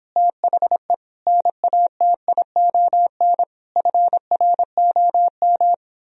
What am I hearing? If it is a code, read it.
Morse code, heard as a keyed tone, about 29 wpm, 700 Hz. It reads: THE NATION FROM